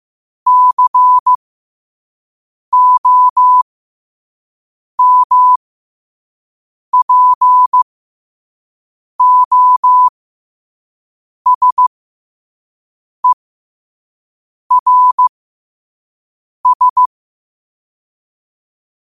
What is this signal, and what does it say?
Morse code, heard as a keyed tone, about 15 wpm, spaced as for 5 wpm, 1000 Hz. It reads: COMPOSERS
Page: Quotes for Tue, 12 Aug 2025 in Morse Code at 5 words per minute.